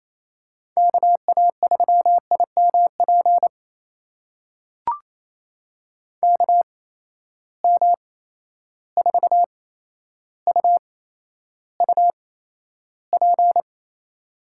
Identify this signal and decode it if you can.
Morse code, heard as a keyed tone, about 28 wpm, spaced as for 12 wpm, 700 Hz. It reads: KA3IMP KM4UUP
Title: USA Call Signs - 6 Characters - 8x Character Spacing - 28wpm